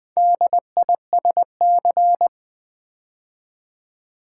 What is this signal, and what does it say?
Morse code, heard as a keyed tone, about 20 wpm, 700 Hz. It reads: DISC
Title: Build A Word - 2 Letters At A Time - 8-Letter Words - 20wpm